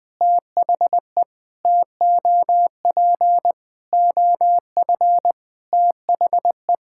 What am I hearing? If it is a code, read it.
Morse code, heard as a keyed tone, about 20 wpm, 700 Hz. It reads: THE TOP OF THE
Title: Sentences with 5-Word Phrases - Spoken Beginning - 20wpm